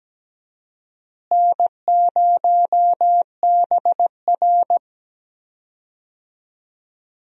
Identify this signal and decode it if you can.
Morse code, heard as a keyed tone, about 17 wpm, 700 Hz. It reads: N0BR
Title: USA Call Signs - 4 Characters - Repeated 3x - 17wpm